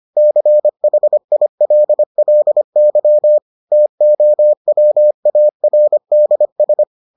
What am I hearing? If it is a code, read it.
Morse code, heard as a keyed tone, about 25 wpm, 600 Hz. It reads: CHILLY TOWARDS